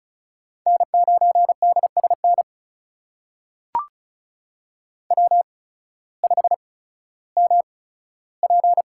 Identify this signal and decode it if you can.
Morse code, heard as a keyed tone, about 36 wpm, spaced as for 16 wpm, 700 Hz. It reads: N9DSN W5MP
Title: USA Call Signs - 8x Character Spacing - 35wpm